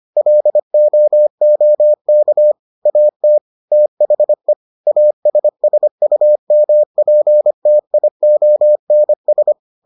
Morse code, heard as a keyed tone, about 25 wpm, 600 Hz. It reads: LOOK AT THE ASSUMPTIONS